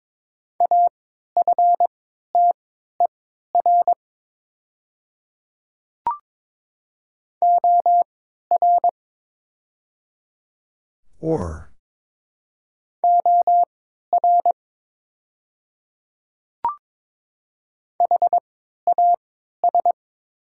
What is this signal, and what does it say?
Morse code, heard as a keyed tone, about 22 wpm, spaced as for 12 wpm, 700 Hz. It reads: AFTER OR OR HAS
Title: 3x Character Spacing - Top 200 Words - 22wpm